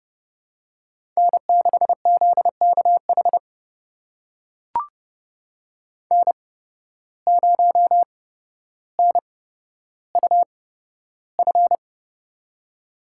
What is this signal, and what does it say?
Morse code, heard as a keyed tone, about 31 wpm, spaced as for 13 wpm, 700 Hz. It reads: N6ZKH N0NUF
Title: USA Call Signs - 5 Characters - 8x Character Spacing - 30wpm